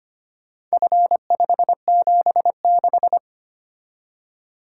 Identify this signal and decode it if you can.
Morse code, heard as a keyed tone, about 25 wpm, 700 Hz. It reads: F576
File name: practice / 25wpm / Four Letter-Number Combinations - 25wpm